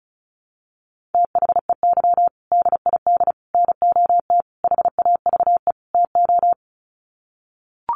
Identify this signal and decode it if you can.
Morse code, heard as a keyed tone, about 34 wpm, 700 Hz. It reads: THEY DID NOT HAVE TO
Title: Sets of 5 Words Encore2 35wpm